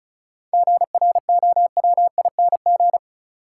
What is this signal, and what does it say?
Morse code, heard as a keyed tone, about 36 wpm, 700 Hz. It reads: GROWING
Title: Sets of 5 Words Encore - New Word Drill - 35wpm